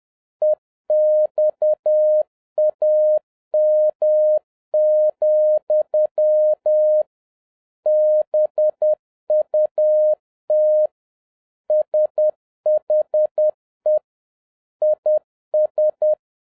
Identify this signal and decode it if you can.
Morse code, heard as a keyed tone, about 10 wpm, 610 Hz. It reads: EXAM, BUT SHE IS